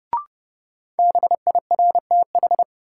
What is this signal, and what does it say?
Morse code, heard as a keyed tone, about 30 wpm, 700 Hz. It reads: BIRTH
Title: Sentences from Top 800 Words - New Word Drill - 30wpm